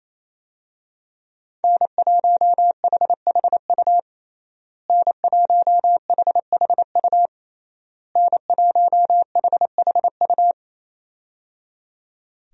Morse code, heard as a keyed tone, about 28 wpm, 700 Hz. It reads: N1HHU N1HHU N1HHU
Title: USA Call Signs - 5 Characters - Repeated 3x - 28wpm